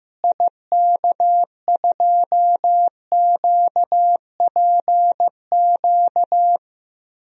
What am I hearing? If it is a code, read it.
Morse code, heard as a keyed tone, about 15 wpm, 700 Hz. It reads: IK2QPQ
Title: DXCC-100 Call Signs - 6 Characters - Repeated 2x - 15wpm